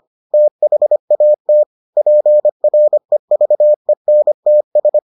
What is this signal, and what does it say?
Morse code, heard as a keyed tone, about 25 wpm, 600 Hz. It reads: THAT PREVENTS